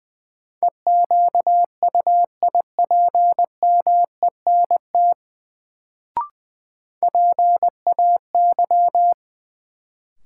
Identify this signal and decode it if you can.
Morse code, heard as a keyed tone, about 20 wpm, 700 Hz. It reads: EQUIPMENT PAY
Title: Sentences from Top 700 Words - New Word Drill - 20wpm